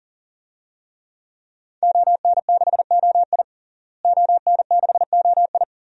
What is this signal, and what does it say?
Morse code, heard as a keyed tone, about 40 wpm, 700 Hz. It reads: ON6OI ON6OI
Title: DXCC-100 Call Signs - 5 Characters - Repeated 2x - 40wpm